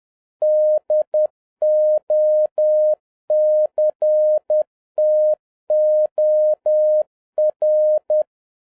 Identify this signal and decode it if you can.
Morse code, heard as a keyed tone, about 10 wpm, 610 Hz. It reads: DOCTOR
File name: 2011-07-mp3 / n56 BACK FROM A BIG ACCIDENT 10wpm